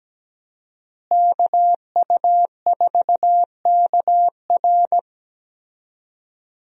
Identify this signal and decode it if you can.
Morse code, heard as a keyed tone, about 17 wpm, 700 Hz. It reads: KU4KR
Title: USA Call Signs - 5 Characters - Repeated 3x - 17wpm